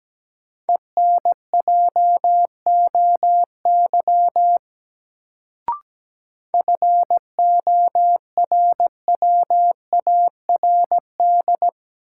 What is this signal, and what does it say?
Morse code, heard as a keyed tone, about 17 wpm, 700 Hz. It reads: ENJOY FORWARD
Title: Sentences from Top 1300 Words - New Word Drill Alt - 17wpm